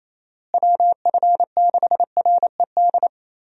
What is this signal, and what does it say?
Morse code, heard as a keyed tone, about 28 wpm, 700 Hz. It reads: WF6RED
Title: USA Call Signs - 6 Characters - Repeated 3x - 28wpm